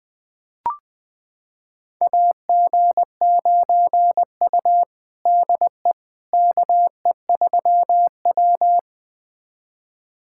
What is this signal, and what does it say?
Morse code, heard as a keyed tone, about 20 wpm, 700 Hz. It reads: AG9U DE KE3W
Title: USA Callsign DE Callsign - 4 Characters - 20wpm